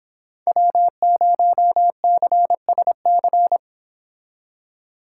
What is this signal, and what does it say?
Morse code, heard as a keyed tone, about 30 wpm, 690 Hz. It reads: W0CSC